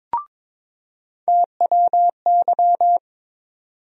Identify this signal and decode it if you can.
Morse code, heard as a keyed tone, about 22 wpm, 700 Hz. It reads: TWY
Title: USA Call Sign Postfixes - 3 Letters 22wpm